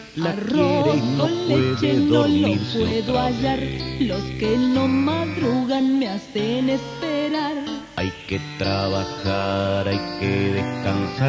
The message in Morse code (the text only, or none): none